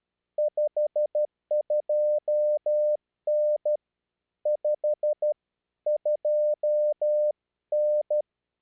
Morse code, heard as a keyed tone, about 12 wpm, 600 Hz. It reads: 52N 52N
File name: eme / rw3bp